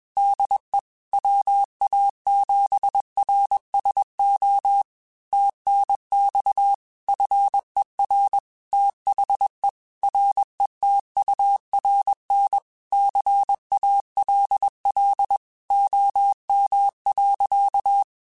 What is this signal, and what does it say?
Morse code, heard as a keyed tone, about 21 wpm, 800 Hz. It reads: DE WA7RSO TNX FER THE RETURN CALL OM.
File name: QSOs / QSOs-mp3s / kc7sac@20